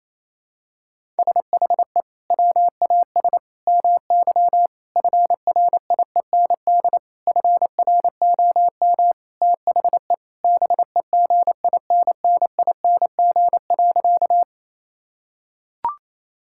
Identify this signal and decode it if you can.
Morse code, heard as a keyed tone, about 28 wpm, 700 Hz. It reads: SHE WAS MY FRIEND FROM THE BEGINNING.
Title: Sentences from Top 1000 Words - Review Part 5 - 28wpm